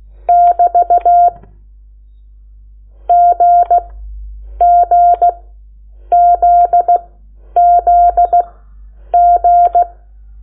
Morse code, heard as a keyed tone, about 16 wpm, spaced as for 8 wpm, 700 Hz. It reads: <BT> GGZZG
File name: Lecciones / CLASE Nº8